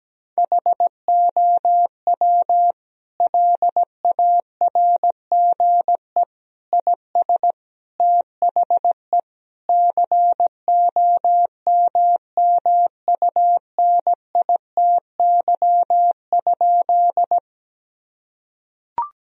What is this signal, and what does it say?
Morse code, heard as a keyed tone, about 17 wpm, 700 Hz. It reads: HOW LARGE IS THE COMMUNITY?
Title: Sentences from Top 1000 Words - Review Part 4 - 17wpm